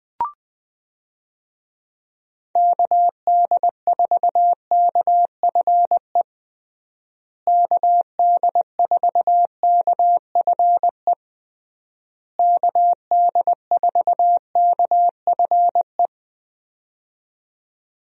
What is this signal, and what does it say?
Morse code, heard as a keyed tone, about 20 wpm, 700 Hz. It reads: KD4KFE KD4KFE KD4KFE
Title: USA Call Signs - 6 Characters - Repeated 3x - 20wpm